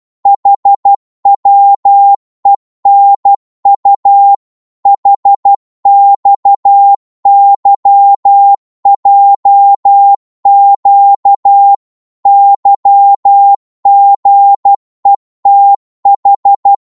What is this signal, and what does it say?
Morse code, heard as a keyed tone, about 12 wpm, 800 Hz. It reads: HWENU HXYJQ YGETH
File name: Morse / ltr-12wpm